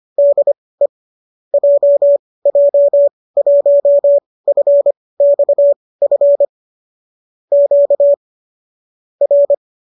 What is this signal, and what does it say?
Morse code, heard as a keyed tone, about 25 wpm, spaced as for 18 wpm, 570 Hz. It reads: DE JJ1FXF Q R